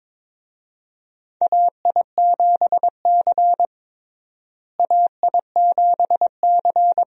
Morse code, heard as a keyed tone, about 22 wpm, 700 Hz. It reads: AI7C AI7C
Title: USA Call Signs - 4 Characters - Repeated 2x - 22wpm